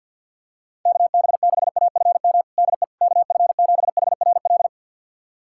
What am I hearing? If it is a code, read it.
Morse code, heard as a keyed tone, about 48 wpm, 700 Hz. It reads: KB6AFM DE KF7HRL
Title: USA Callsign DE Callsign - 6 Characters - 50wpm